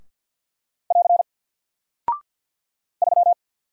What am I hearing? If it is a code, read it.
Morse code, heard as a keyed tone, about 49 wpm, 700 Hz. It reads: <AR> 3
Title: Single Character (Letter:Number:Punctuation:Prosign) Alt 50wpm